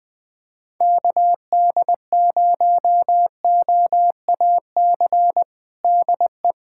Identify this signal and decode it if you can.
Morse code, heard as a keyed tone, about 20 wpm, 700 Hz. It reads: KD0OAC DE
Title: USA Callsign DE Callsign 20wpm